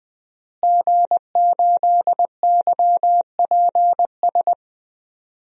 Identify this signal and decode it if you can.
Morse code, heard as a keyed tone, about 20 wpm, 700 Hz. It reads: G8YPS
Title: DXCC-100 Call Signs - 5 Characters - Repeated 2x - 20wpm